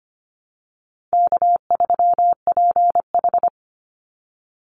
Morse code, heard as a keyed tone, about 25 wpm, 700 Hz. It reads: K3PH
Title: Call Signs - 25wpm